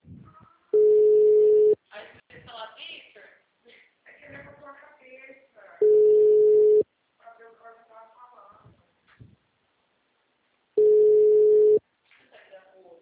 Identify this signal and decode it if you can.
Ringback tone (Europe, 425 Hz), with speech.